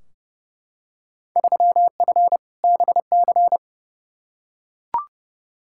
Morse code, heard as a keyed tone, about 30 wpm, 700 Hz. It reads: 3F BC
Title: Field Day Exchanges Encore - 30wpm